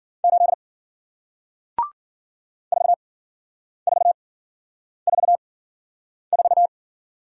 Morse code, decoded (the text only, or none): 8 4444